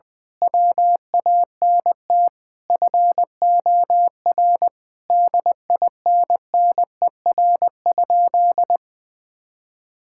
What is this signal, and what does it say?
Morse code, heard as a keyed tone, about 20 wpm, 700 Hz. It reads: WANT FOR DINNER?
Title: Sentences with 4-Word Phrases - Spoken Beginning - 20wpm